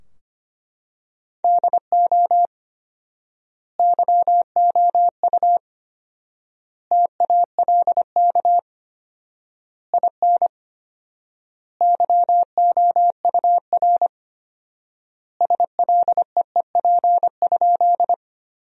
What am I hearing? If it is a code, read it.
Morse code, heard as a keyed tone, about 25 wpm, 700 Hz. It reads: DO YOU TALK IN YOUR SLEEP?
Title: 4x Word-Spacing - Sentences from Top 500 Words - 25wpm